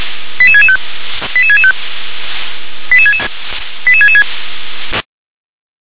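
Five-tone selective call (ZVEI), heard as quick runs of five tones, tones of about 70 ms. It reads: 88665 88665 88666
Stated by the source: ZVEI five-tone selective call 88747 received by SDR.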